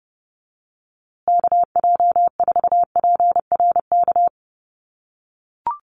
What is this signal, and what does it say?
Morse code, heard as a keyed tone, about 30 wpm, 700 Hz. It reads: KJ4PRK
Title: USA Call Signs Repeated 3x 30wpm